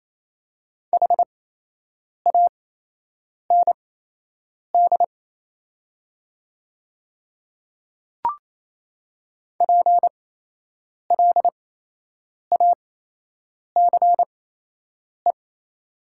Morse code, heard as a keyed tone, about 28 wpm, spaced as for 8 wpm, 700 Hz. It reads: HAND PLACE
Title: 8x Character Spacing - Top 200 Words - 28wpm